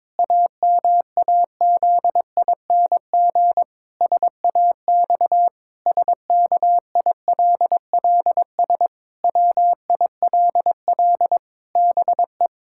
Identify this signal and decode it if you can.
Morse code, heard as a keyed tone, about 22 wpm, 700 Hz. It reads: AMAZING SAX SKILLS WILL BE
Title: Pangram Challenge v2 - 22wpm